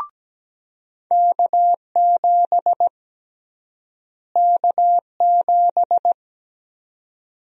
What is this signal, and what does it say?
Morse code, heard as a keyed tone, about 17 wpm, 700 Hz. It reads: K7 K7
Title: USA Call Sign Prefixes - Repeated 2x - 17wpm